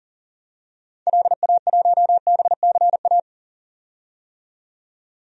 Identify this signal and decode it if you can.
Morse code, heard as a keyed tone, about 40 wpm, 700 Hz. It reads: LA1BCA
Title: DXCC-100 Call Signs - 6 Characters - 40wpm